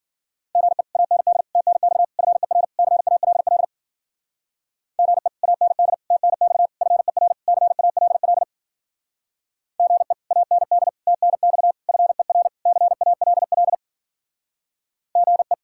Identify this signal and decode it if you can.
Morse code, heard as a keyed tone, about 52 wpm, 700 Hz. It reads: GE AND TNX FER CALL GE AND TNX FER CALL GE AND TNX FER CALL GE